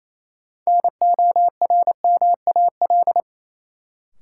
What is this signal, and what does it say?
Morse code, heard as a keyed tone, about 27 wpm, 700 Hz. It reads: NORMAL
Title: Sentences from Top 1200 Words - New Word Drill Alt - 28wpm